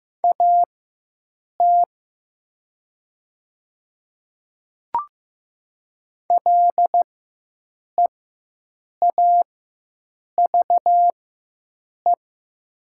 Morse code, heard as a keyed tone, about 15 wpm, spaced as for 7 wpm, 700 Hz. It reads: AT LEAVE
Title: 4x Character Spacing - Top 500 Words - 15wpm